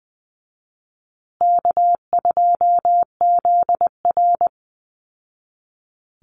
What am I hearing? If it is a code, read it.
Morse code, heard as a keyed tone, about 20 wpm, 700 Hz. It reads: K2ZR
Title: Easy Call Signs Repeated 2x - 20wpm